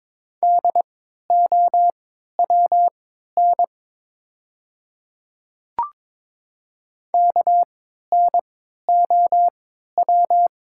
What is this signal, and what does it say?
Morse code, heard as a keyed tone, about 22 wpm, spaced as for 12 wpm, 700 Hz. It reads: DOWN KNOW
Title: 3x Character Spacing - Top 100 Words - 22wpm